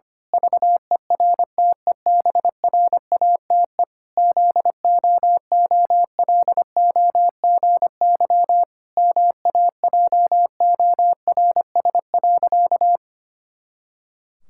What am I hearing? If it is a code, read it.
Morse code, heard as a keyed tone, about 25 wpm, 700 Hz. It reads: VERTEBRATE ZOOLOGY MAJORS.
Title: Pangram Challenge v2 - 25wpm